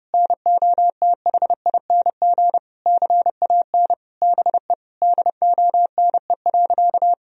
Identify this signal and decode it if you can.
Morse code, heard as a keyed tone, about 30 wpm, 700 Hz. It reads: NOTHING CAN BE DONE.
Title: Sentences from Top 1000 Words - Review Part 1 - 30wpm